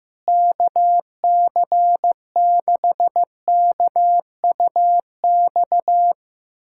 Morse code, heard as a keyed tone, about 15 wpm, 700 Hz. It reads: KC6KUX
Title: USA Callsign DE Callsign - 6 Characters - 15wpm